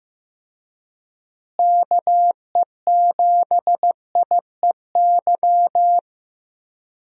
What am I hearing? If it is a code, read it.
Morse code, heard as a keyed tone, about 15 wpm, 700 Hz. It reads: KE7IEY
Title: USA Call Signs - 6 Characters - Repeated 3x - 15wpm